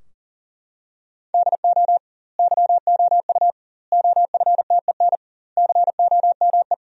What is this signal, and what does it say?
Morse code, heard as a keyed tone, about 40 wpm, 700 Hz. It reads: DO YOU OFTEN COME
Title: 2x Word-Spacing - Sentences from Top 300 Words - 40wpm